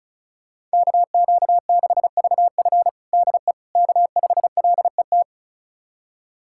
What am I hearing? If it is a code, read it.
Morse code, heard as a keyed tone, about 35 wpm, 690 Hz. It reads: KQ6VF DE K5LET
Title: USA Callsign DE Callsign - 5 Characters - 35wpm